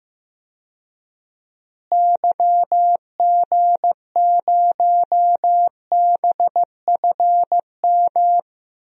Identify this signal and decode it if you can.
Morse code, heard as a keyed tone, about 15 wpm, 700 Hz. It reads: YG0BFM
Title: DXCC-100 Call Signs - 6 Characters - Repeated 2x - 15wpm